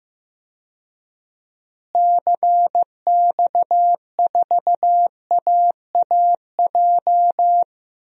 Morse code, heard as a keyed tone, about 15 wpm, 700 Hz. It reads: CX4AAJ
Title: DXCC-100 Call Signs - 6 Characters - Repeated 3x - 15wpm